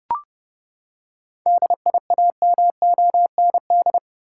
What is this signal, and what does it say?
Morse code, heard as a keyed tone, about 30 wpm, 700 Hz. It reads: DIAMOND
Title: Seven Letter Words - 30wpm